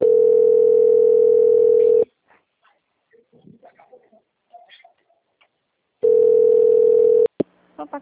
Ringback tone (North America), 440 + 480 Hz.
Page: Guarapuava - PR